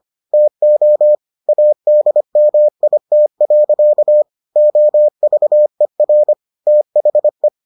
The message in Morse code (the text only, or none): TO ADMIT. OVER THE